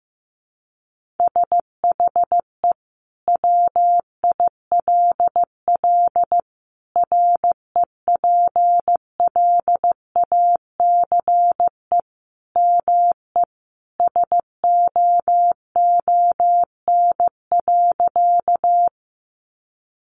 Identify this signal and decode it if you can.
Morse code, heard as a keyed tone, about 14 wpm, 700 Hz. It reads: SHE WILL REPLACE ME SOON.
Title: Sentences from Top 1400 Words 15wpm